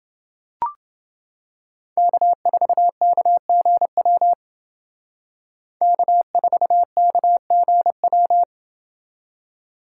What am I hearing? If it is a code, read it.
Morse code, heard as a keyed tone, about 30 wpm, 700 Hz. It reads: K4KGW K4KGW